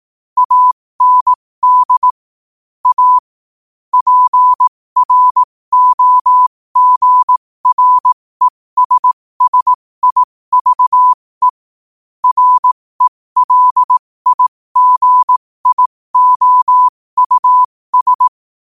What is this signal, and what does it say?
Morse code, heard as a keyed tone, about 18 wpm, spaced as for 15 wpm, 1000 Hz. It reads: AND A PROGRESSIVE RELIGIOUS